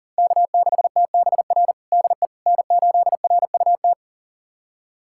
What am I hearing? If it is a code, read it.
Morse code, heard as a keyed tone, about 40 wpm, 700 Hz. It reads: K6TBR DE N8RUT
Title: USA Callsign DE Callsign - 5 Characters - 40wpm